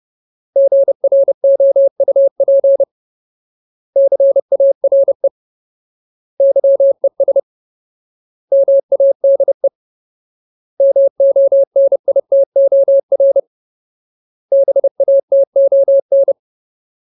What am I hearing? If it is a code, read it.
Morse code, heard as a keyed tone, about 30 wpm, 550 Hz. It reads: GROUP CARE YES MADE MONITOR BATON
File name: Ultimate Word List / 30WPM Ultimate